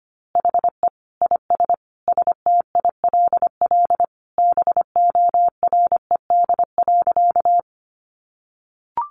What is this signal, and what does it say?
Morse code, heard as a keyed tone, about 25 wpm, 700 Hz. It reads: HE IS STILL BORED.
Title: Easy Phrases 25wpm